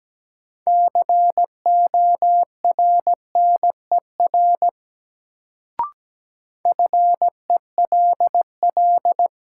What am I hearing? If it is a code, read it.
Morse code, heard as a keyed tone, about 17 wpm, 700 Hz. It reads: CORNER FELL
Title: Sentences from Top 1600 Words - New Word Drill Alt - 17wpm